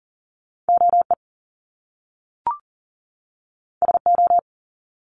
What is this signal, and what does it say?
Morse code, heard as a keyed tone, about 40 wpm, 700 Hz. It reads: OE SO
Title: Top Two Letter Combinations in Words 40wpm